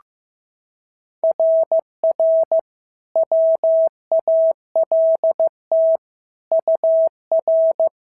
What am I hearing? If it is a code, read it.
Morse code, heard as a keyed tone, about 14 wpm, 650 Hz. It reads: RR WALT UR
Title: SOTA QSO Practice 15wpm